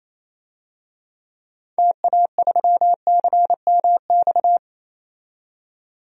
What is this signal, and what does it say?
Morse code, heard as a keyed tone, about 28 wpm, 700 Hz. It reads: TA3CMX
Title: DXCC-100 Call Signs - 6 Characters - 28wpm